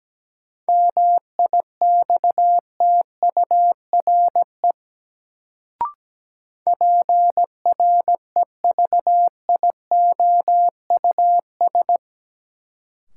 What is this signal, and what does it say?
Morse code, heard as a keyed tone, about 17 wpm, 700 Hz. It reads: MIXTURE PREVIOUS
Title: Sentences from Top 1500 Words - New Word Drill - 17wpm